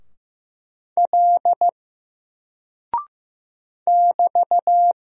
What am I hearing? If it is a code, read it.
Morse code, heard as a keyed tone, about 15 wpm, 700 Hz. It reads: L <BT>
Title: Single Character (Letter:Number:Punctuation:Prosign) Alt 15wpm